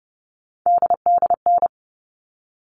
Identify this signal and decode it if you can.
Morse code, heard as a keyed tone, about 30 wpm, 700 Hz. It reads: DDN
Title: All Cut Numbers - 3-Digit 30wpm